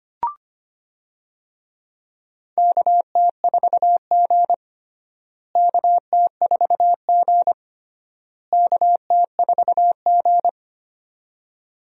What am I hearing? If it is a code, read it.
Morse code, heard as a keyed tone, about 25 wpm, 700 Hz. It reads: KT4G KT4G KT4G
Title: USA Call Signs - 4 Characters - Repeated 3x - 25wpm